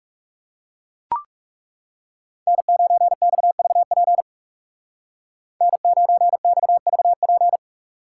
Morse code, heard as a keyed tone, about 35 wpm, 700 Hz. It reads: N9XVP N9XVP
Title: Speed Racing - USA Call Signs 30wpm